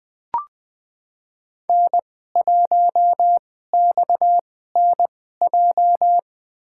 Morse code, heard as a keyed tone, about 20 wpm, spaced as for 15 wpm, 700 Hz. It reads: N1XNJ
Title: USA Call Signs - 5 Characters - 2x Character Spacing - 20wpm